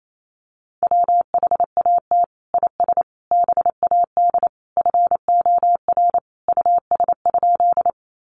Morse code, heard as a keyed tone, about 28 wpm, 700 Hz. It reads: WHAT IS BAD FOR US?
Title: Sentences from Top 700 Words 28wpm